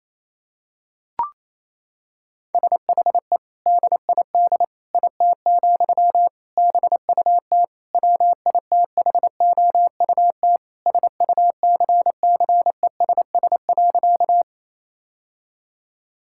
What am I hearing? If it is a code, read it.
Morse code, heard as a keyed tone, about 28 wpm, 700 Hz. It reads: SHE DID IT, BUT WITHOUT SUCCESS.
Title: Sentences from Top 1000 Words - Review Part 6 - 28wpm